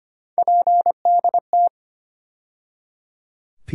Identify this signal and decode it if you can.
Morse code, heard as a keyed tone, about 25 wpm, 700 Hz. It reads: PDT
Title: USA Call Sign Postfixes - 3 Letters 25wpm